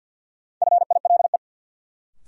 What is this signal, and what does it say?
Morse code, heard as a keyed tone, about 52 wpm, 700 Hz. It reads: FILE
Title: Sentences from Top 1200 Words - New Word Drill Alt - 50wpm